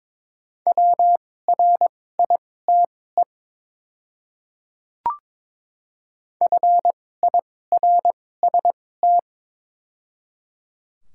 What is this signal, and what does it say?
Morse code, heard as a keyed tone, about 22 wpm, spaced as for 16 wpm, 700 Hz. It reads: WRITE FIRST
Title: 2x Character Spacing - Top 200 Words - 22wpm